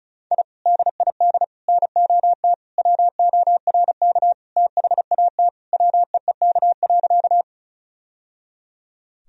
Morse code, heard as a keyed tone, about 36 wpm, 700 Hz. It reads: I DID NOT WORK THAT WEEK.
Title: Sentences from Top 1000 Words - Review Part 2 - 35wpm